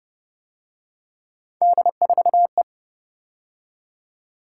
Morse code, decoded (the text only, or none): D4E